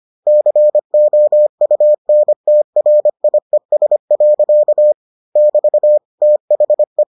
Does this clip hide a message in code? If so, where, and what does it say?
Yes: Morse code from 0.3 to 7.0 s, heard as a keyed tone, about 25 wpm, 600 Hz. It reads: COUNTRIES. <BT> THE